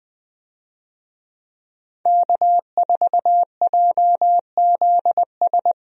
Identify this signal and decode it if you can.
Morse code, heard as a keyed tone, about 20 wpm, 700 Hz. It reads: K4JZS
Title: USA Call Signs - 5 Characters - 20wpm